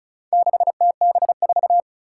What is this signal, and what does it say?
Morse code, heard as a keyed tone, about 34 wpm, 700 Hz. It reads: 6TB4
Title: Four Challenging Letter-Number Combinations 35wpm